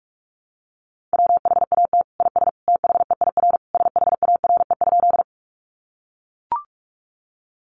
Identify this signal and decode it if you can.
Morse code, heard as a keyed tone, about 44 wpm, 700 Hz. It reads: WHAT IS THEIR SHARE?
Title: Sentences from Top 900 Words 45wpm